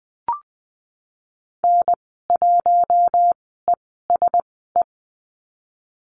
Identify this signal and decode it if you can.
Morse code, heard as a keyed tone, about 19 wpm, spaced as for 14 wpm, 700 Hz. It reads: N1ESE
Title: 055 Lesson 10g - Letter r - Call Signs 20wpm